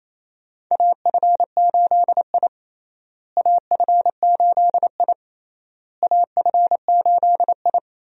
Morse code, heard as a keyed tone, about 28 wpm, 700 Hz. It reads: AF8I AF8I AF8I
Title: USA Call Signs - 4 Characters - Repeated 3x - 28wpm